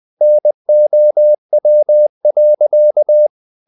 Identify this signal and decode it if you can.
Morse code, heard as a keyed tone, about 20 wpm, 600 Hz. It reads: NOW.
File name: the old man and the sea part 2 20wpm